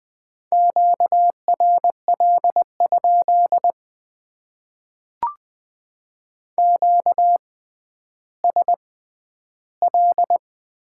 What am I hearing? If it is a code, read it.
Morse code, heard as a keyed tone, about 20 wpm, spaced as for 11 wpm, 700 Hz. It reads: QRL? QSL
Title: Top 90 QSO Elements - 6x Character Spacing - 20wpm